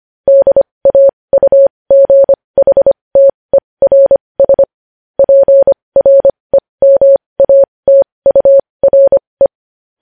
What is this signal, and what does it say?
Morse code, heard as a keyed tone, about 25 wpm, spaced as for 20 wpm, 550 Hz. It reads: DAUGHTERS PREMATURE
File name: podcast / News headlines-1x-25x20